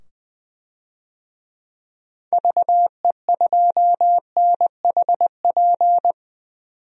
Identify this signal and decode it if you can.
Morse code, heard as a keyed tone, about 20 wpm, 700 Hz. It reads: VE2NHP
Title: DXCC-100 Call Signs - 6 Characters - 20wpm